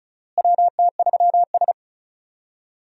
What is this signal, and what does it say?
Morse code, heard as a keyed tone, about 36 wpm, 700 Hz. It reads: WT3S
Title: USA Call Signs - 4 Characters - Repeated 3x - 35wpm